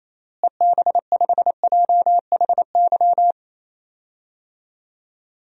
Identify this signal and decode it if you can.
Morse code, heard as a keyed tone, about 28 wpm, 700 Hz. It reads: EB5JHY
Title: DXCC-100 Call Signs - 6 Characters - Repeated 3x - 28wpm